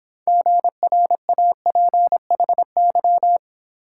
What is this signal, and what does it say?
Morse code, heard as a keyed tone, about 27 wpm, 700 Hz. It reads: GRAPHY
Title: Speed-Racing Multi-letter Phonemes - 22wpm